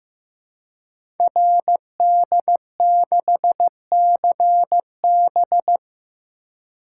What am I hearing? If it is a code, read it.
Morse code, heard as a keyed tone, about 15 wpm, 700 Hz. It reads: RD6CB
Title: Code Groups - 5 Letter:Numbers - 15wpm